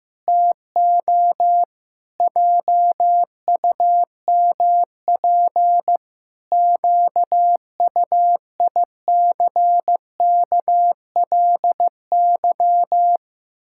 Morse code, heard as a keyed tone, about 15 wpm, 700 Hz. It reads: TO JUMP QUICKLY
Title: Pangram Challenge v2 - 15wpm